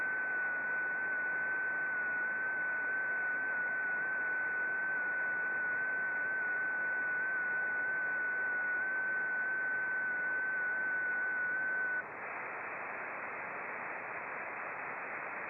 FT8.
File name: contests / sept-vhf-k5tr- -0300